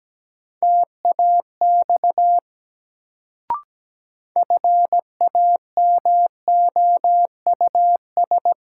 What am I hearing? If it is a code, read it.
Morse code, heard as a keyed tone, about 17 wpm, 700 Hz. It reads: TAX FAMOUS
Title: Sentences from Top 1500 Words - New Word Drill Alt - 17wpm